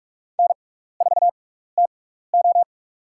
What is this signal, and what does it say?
Morse code, heard as a keyed tone, about 46 wpm, spaced as for 15 wpm, 700 Hz. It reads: N4TO
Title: USA Call Signs - 4 Characters - 6x Character Spacing - 45wpm